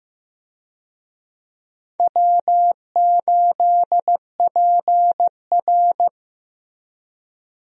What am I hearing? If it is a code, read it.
Morse code, heard as a keyed tone, about 15 wpm, 700 Hz. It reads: W8PR
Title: USA Call Signs - 4 Characters - Repeated 2x - 15wpm